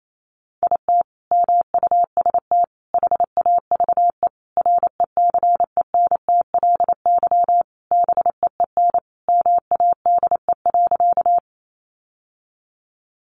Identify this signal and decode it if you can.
Morse code, heard as a keyed tone, about 28 wpm, 700 Hz. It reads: IT MUST HAVE RECENTLY BEEN MADE.
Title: Sentences from Top 1200 Words 28wpm